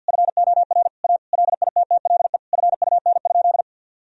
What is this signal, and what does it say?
Morse code, heard as a keyed tone, about 51 wpm, 700 Hz. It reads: FOR A LITTLE FUN?
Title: Sentences from Top 1000 Words - Review Part 1 - 50wpm